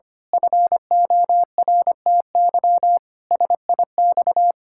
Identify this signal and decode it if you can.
Morse code, heard as a keyed tone, about 25 wpm, 700 Hz. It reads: FORTY SIX
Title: Pangram Challenge v2 - 25wpm